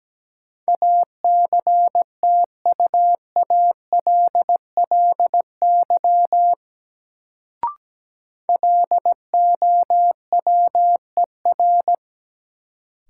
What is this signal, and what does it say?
Morse code, heard as a keyed tone, about 17 wpm, 700 Hz. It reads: ACTUALLY LOWER
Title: Sentences from Top 700 Words - New Word Drill - 17wpm